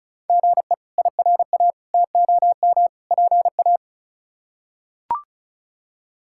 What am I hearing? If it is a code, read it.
Morse code, heard as a keyed tone, about 36 wpm, 700 Hz. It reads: GE IRA TOM PA
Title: K1USN SST QSO Practice - 35wpm